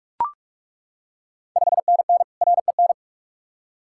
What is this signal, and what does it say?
Morse code, heard as a keyed tone, about 41 wpm, 700 Hz. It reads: 5NN REN
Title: Ontario QSO Party Exchange 45wpm